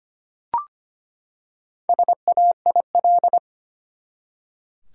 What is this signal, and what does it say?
Morse code, heard as a keyed tone, about 25 wpm, 700 Hz. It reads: SAIL
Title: Four Letter Words - 25wpm